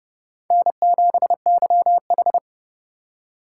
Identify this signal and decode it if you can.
Morse code, heard as a keyed tone, about 29 wpm, 700 Hz. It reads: N7YH